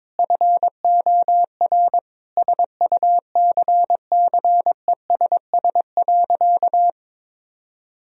Morse code, heard as a keyed tone, about 22 wpm, 700 Hz. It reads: FOR SUCCESS.
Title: Sentences from Top 1000 Words - Review Part 3 - 22wpm